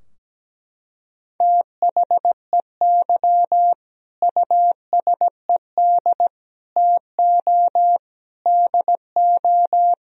Morse code, heard as a keyed tone, about 17 wpm, 700 Hz. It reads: THEY USED TO DO
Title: Sentences from Top 1000 Words - Review Part 4 - 17wpm